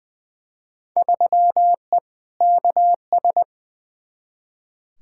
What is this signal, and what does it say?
Morse code, heard as a keyed tone, about 20 wpm, 700 Hz. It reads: 3E KS
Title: Field Day Exchanges Encore - 20wpm